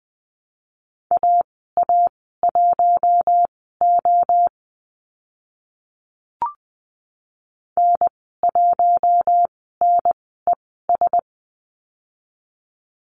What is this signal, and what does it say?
Morse code, heard as a keyed tone, about 20 wpm, spaced as for 14 wpm, 700 Hz. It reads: AA1O N1NES
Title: 042 Lesson 8g - Number 1 - Call Signs 20wpm